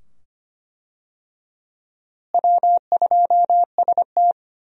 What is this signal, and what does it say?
Morse code, heard as a keyed tone, about 25 wpm, 700 Hz. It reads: W2ST
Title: USA Call Signs - 4 Characters - Repeated 2x - 25wpm